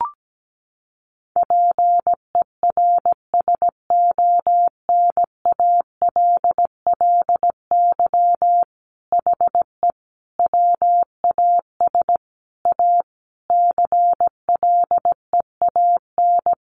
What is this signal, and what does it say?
Morse code, heard as a keyed tone, about 17 wpm, 700 Hz. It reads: PERSONALLY HE WAS A CLEAN
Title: Sentences from Top 1400 Words 17wpm